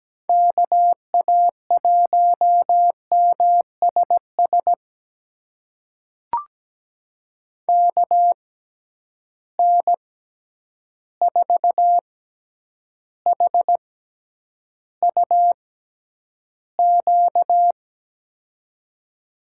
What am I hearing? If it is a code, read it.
Morse code, heard as a keyed tone, about 17 wpm, spaced as for 9 wpm, 700 Hz. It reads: KA1MSS KN4HUQ
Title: USA Call Signs - 6 Characters - 6x Character Spacing - 17wpm